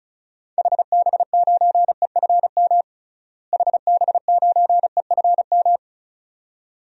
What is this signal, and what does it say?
Morse code, heard as a keyed tone, about 36 wpm, 700 Hz. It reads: HB9EFM HB9EFM
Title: DXCC-100 Call Signs - 6 Characters - Repeated 2x - 35wpm